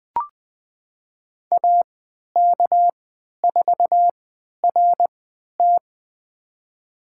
Morse code, heard as a keyed tone, about 21 wpm, spaced as for 11 wpm, 700 Hz. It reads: AK4RT
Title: USA Call Signs - 5 Characters - 3x Character Spacing - 20wpm